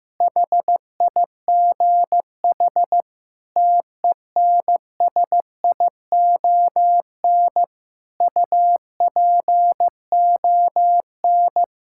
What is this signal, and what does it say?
Morse code, heard as a keyed tone, about 15 wpm, 700 Hz. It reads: HIGH TENSION UPON
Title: Sentences from Top 1600 Words 15wpm